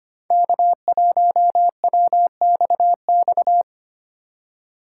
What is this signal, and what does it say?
Morse code, heard as a keyed tone, about 25 wpm, 700 Hz. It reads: K1WXX
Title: USA Call Signs - 4x Character Spacing - 25wpm